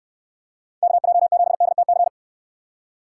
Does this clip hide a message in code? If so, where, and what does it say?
Yes: Morse code from 0.8 to 2.1 s, heard as a keyed tone, about 66 wpm, 700 Hz. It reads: KJ7DTC